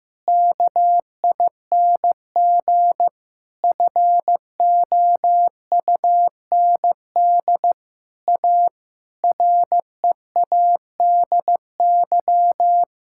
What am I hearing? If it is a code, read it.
Morse code, heard as a keyed tone, about 16 wpm, 700 Hz. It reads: KING FOUND A READY
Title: Sentences from Top 1000 Words - Review Part 5 - 15wpm